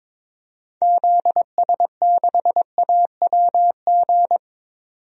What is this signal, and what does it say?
Morse code, heard as a keyed tone, about 22 wpm, 700 Hz. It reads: ZS6AWG
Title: DXCC-100 Call Signs - 6 Characters - 22wpm